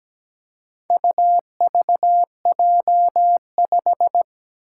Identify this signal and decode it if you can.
Morse code, heard as a keyed tone, about 17 wpm, 700 Hz. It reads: UVJ5
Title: Four Challenging Letter-Number Combinations 17wpm